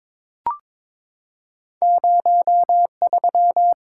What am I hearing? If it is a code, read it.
Morse code, heard as a keyed tone, about 23 wpm, 700 Hz. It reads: 03
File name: practice / 22wpm / Two Letter-Number Combinations - 22wpm